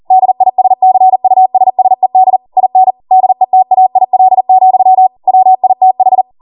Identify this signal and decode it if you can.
Morse code, heard as a keyed tone, about 40 wpm, 750 Hz. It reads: DISCUSSED IN DETAIL, WITH